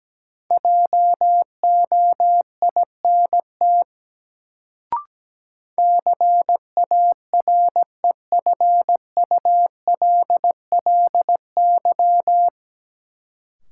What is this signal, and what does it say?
Morse code, heard as a keyed tone, about 17 wpm, 700 Hz. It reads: JOINT CAREFULLY
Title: Sentences from Top 1300 Words - New Word Drill - 17wpm